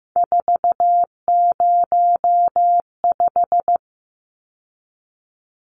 Morse code, heard as a keyed tone, about 15 wpm, 700 Hz. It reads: 405
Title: Numbers - 3-Digit 15wpm